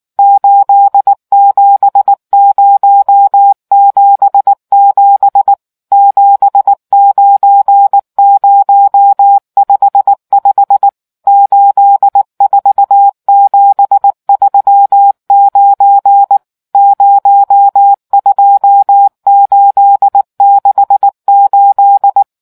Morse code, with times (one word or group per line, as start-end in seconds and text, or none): 0.2-5.5 87077
5.9-10.9 79055
11.3-16.4 84739
16.8-22.2 02868